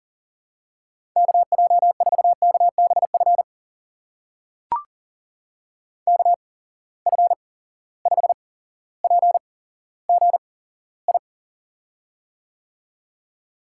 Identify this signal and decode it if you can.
Morse code, heard as a keyed tone, about 39 wpm, spaced as for 17 wpm, 700 Hz. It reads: KJ4KBF KF5PGI
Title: USA Call Signs - 6 Characters - 8x Character Spacing - 40wpm